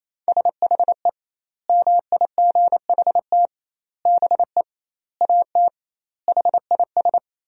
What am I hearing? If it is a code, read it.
Morse code, heard as a keyed tone, about 28 wpm, 700 Hz. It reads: SHE MIGHT BE AT HIS
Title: 2x Word-Spacing - Sentences from Top 300 Words - 28wpm